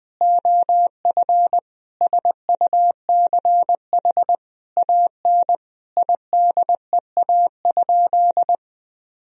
Morse code, heard as a keyed tone, about 20 wpm, 700 Hz. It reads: OF SUCH AN IDEA?
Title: Sentences from Top 400 Words and Abbreviations 20wpm